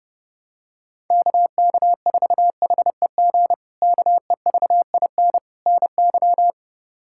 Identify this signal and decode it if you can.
Morse code, heard as a keyed tone, about 30 wpm, 700 Hz. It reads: KK4HEG KEVIN NY
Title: K1USN SST QSO Practice - 30wpm